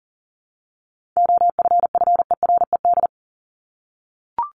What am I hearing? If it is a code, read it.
Morse code, heard as a keyed tone, about 40 wpm, 700 Hz. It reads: OFFERED
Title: ED Suffix - Top 100 Words with Base Words 40wpm